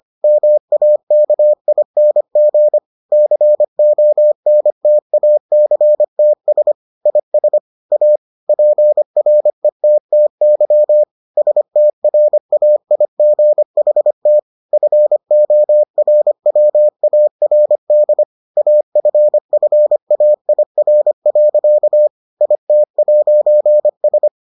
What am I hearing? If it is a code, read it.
Morse code, heard as a keyed tone, about 25 wpm, 600 Hz. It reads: MAKING CONTACTS IS A PRETTY STRAIGHT FORWARD AFFAIR. IT'S